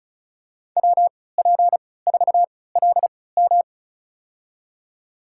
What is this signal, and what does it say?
Morse code, heard as a keyed tone, about 36 wpm, spaced as for 20 wpm, 700 Hz. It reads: WP4LM
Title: USA Call Signs - 3x Character Spacing - 35wpm